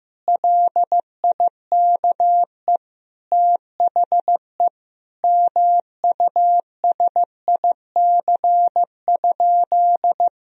Morse code, heard as a keyed tone, about 15 wpm, 700 Hz. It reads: LIKE THE MUSIC?
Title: Sentences from Top 1000 Words - Review Part 1 - 15wpm